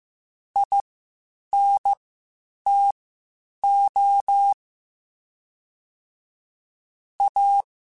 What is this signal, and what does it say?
Morse code, heard as a keyed tone, about 15 wpm, spaced as for 8 wpm, 800 Hz. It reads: INTO A
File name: QSOs / QSOs-mp3s / WD0BGP 1@10